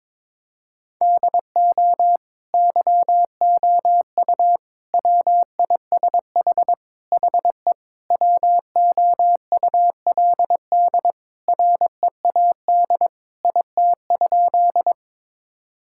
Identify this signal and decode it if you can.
Morse code, heard as a keyed tone, about 22 wpm, 700 Hz. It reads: DO YOU WISH HE WOULD READ IT?
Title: Sentences from Top 1600 Words 22wpm